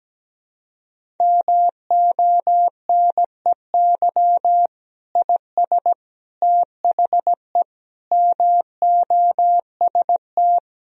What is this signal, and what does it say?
Morse code, heard as a keyed tone, about 17 wpm, 700 Hz. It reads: MONEY IS THE MOST
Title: Sentences from Top 1000 Words - Review Part 1 - 17wpm